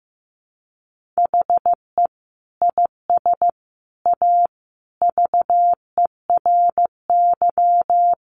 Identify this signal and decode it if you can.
Morse code, heard as a keyed tone, about 15 wpm, 700 Hz. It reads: HE IS A VERY
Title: Sentences from Top 1200 Words 15wpm